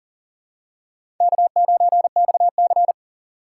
Morse code, heard as a keyed tone, about 40 wpm, 700 Hz. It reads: K9XC
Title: USA Call Signs - 4 Characters - 2x Character Spacing - 40wpm